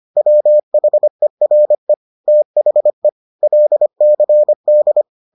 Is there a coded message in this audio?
Yes: Morse code from 0.2 to 5.0 s, heard as a keyed tone, about 25 wpm, 600 Hz. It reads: WHERE THE LCD